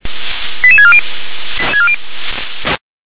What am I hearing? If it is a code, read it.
Five-tone selective call (ZVEI), heard as quick runs of five tones, tones of about 70 ms. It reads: 88640
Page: /357/SDR/ZVEI/88702/